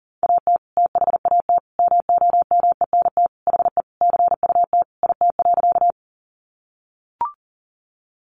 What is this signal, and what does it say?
Morse code, heard as a keyed tone, about 40 wpm, 700 Hz. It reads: AT THAT MOMENT HE CUT IT.
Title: Sentences from Top 1000 Words 40wpm